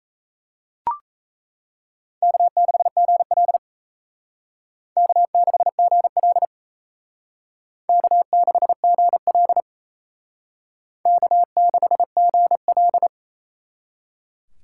Morse code, heard as a keyed tone, about 35 wpm, 700 Hz. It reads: K6GL K6GL K6GL K6GL